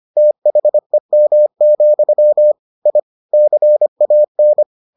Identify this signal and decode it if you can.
Morse code, heard as a keyed tone, about 25 wpm, 600 Hz. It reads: THEM, I CAN